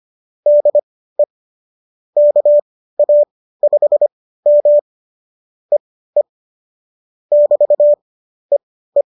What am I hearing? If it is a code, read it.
Morse code, heard as a keyed tone, about 25 wpm, spaced as for 15 wpm, 590 Hz. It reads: DE KA5M EE <BT> EE